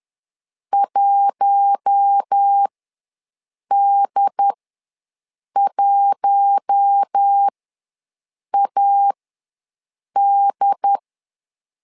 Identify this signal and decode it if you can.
Morse code, heard as a keyed tone, about 10 wpm, spaced as for 6 wpm, 800 Hz. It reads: 1D1AD
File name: LEZIONI / 4 Q1 ws 10wpm 2538